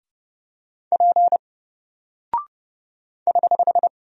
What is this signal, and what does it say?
Morse code, heard as a keyed tone, about 30 wpm, 700 Hz. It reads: P <HH>
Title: Single Character (Letter:Number:Punctuation:Prosign) 30wpm